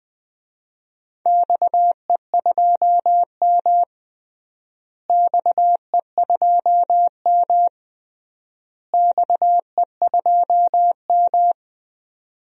Morse code, heard as a keyed tone, about 20 wpm, 700 Hz. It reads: XE2M XE2M XE2M
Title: DXCC-100 Call Signs - 4 Characters - Repeated 3x - 20wpm